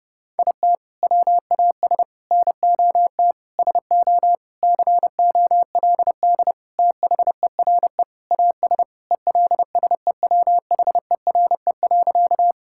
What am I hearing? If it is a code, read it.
Morse code, heard as a keyed tone, about 30 wpm, 700 Hz. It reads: IT WAS NOT SO COLD THERE AS ELSEWHERE.
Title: Sentences from Top 1600 Words 30wpm